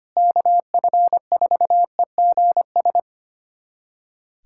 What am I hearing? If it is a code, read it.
Morse code, heard as a keyed tone, about 25 wpm, 700 Hz. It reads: KF4EGS
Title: USA Callsign DE Callsign - 6 Characters - 25wpm